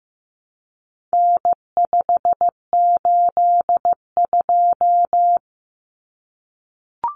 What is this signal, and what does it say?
Morse code, heard as a keyed tone, about 15 wpm, 700 Hz. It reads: N582
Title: Common Cut Numbers - 4-Digit 15wpm